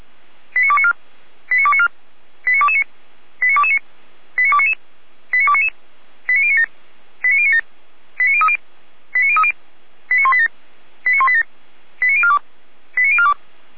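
Five-tone selective call (ZVEI), heard as quick runs of five tones, tones of about 70 ms. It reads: 89294 89294 89229 89229 89299 89299 89097 89097 89030 89030 89187 89187 89042 89042
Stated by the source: Recorded with an SDR, ZVEI five-tone selective call for the Orlamünde fire brigade (Feuerwehr Orlamünde).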